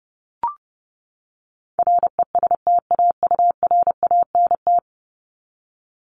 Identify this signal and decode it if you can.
Morse code, heard as a keyed tone, about 30 wpm, 700 Hz. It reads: RESTAURANT
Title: Top 1751-2000 Words 30wpm